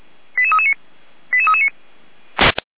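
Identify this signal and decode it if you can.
Five-tone selective call (ZVEI), heard as quick runs of five tones, tones of about 70 ms. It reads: 88339 88339
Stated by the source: ZVEI five-tone call 88341 received by SDR, recorded by feuerwehr-orlamuende.org.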